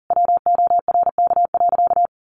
Morse code, heard as a keyed tone, about 40 wpm, 700 Hz. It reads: WORK.
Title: Sentences from Top 1400 Words 40wpm